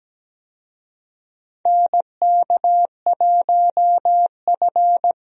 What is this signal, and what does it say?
Morse code, heard as a keyed tone, about 17 wpm, 700 Hz. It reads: NK1F
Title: USA Call Signs - 4 Characters - 17wpm